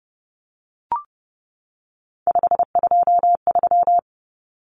Morse code, heard as a keyed tone, about 30 wpm, 700 Hz. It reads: 523
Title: Numbers - 3-Digit 30wpm